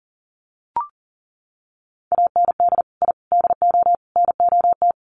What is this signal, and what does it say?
Morse code, heard as a keyed tone, about 40 wpm, 700 Hz. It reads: AND I DO NOT
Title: Sets of 4 Words 40wpm